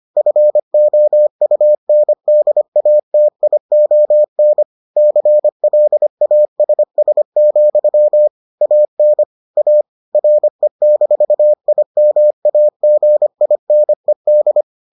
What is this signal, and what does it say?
Morse code, heard as a keyed tone, about 25 wpm, 600 Hz. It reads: FOUNDATION CLASS, AN A RE-IMAGINED